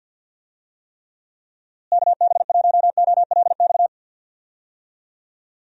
Morse code, heard as a keyed tone, about 45 wpm, 700 Hz. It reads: KB1OLX
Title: USA Call Signs - 6 Characters - Repeated 2x - 50wpm